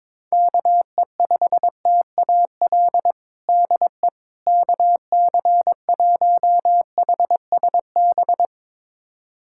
Morse code, heard as a keyed tone, about 22 wpm, 700 Hz. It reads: KE5TAL DE KC1HSB
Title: USA Callsign DE Callsign - 6 Characters - 22wpm